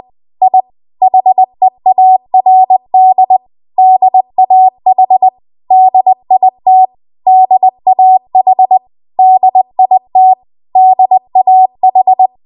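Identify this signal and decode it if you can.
Morse code, heard as a keyed tone, about 20 wpm, 750 Hz. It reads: I HEARD DAH DIT DAH DIT DAH
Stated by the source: Morse code practice at 20 WPM.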